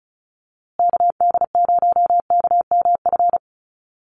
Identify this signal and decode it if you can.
Morse code, heard as a keyed tone, about 34 wpm, 700 Hz. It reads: KD0KMF
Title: USA Call Signs Repeated 2x 35wpm